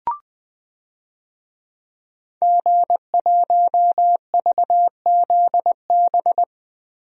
Morse code, heard as a keyed tone, about 21 wpm, 700 Hz. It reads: G1VZB
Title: DXCC-100 Call Signs - 5 Characters - Repeated 3x - 20wpm